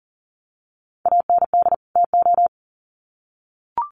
Morse code, heard as a keyed tone, about 40 wpm, 700 Hz. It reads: AND TO
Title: Sets of 2 Words 40wpm